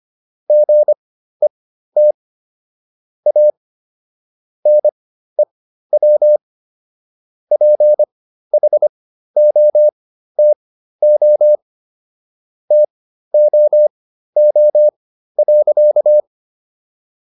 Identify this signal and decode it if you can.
Morse code, heard as a keyed tone, about 25 wpm, spaced as for 13 wpm, 600 Hz. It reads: GET A NEW PHOTO TOO.